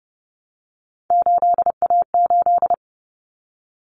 Morse code, heard as a keyed tone, about 30 wpm, 700 Hz. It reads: 8A8
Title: Common Cut Numbers - 3-Digit 30wpm